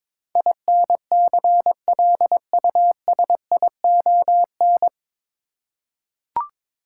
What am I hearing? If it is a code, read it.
Morse code, heard as a keyed tone, about 21 wpm, 700 Hz. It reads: INCLUSION
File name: 9 Letter Words - First 3 Letters Given - 22wpm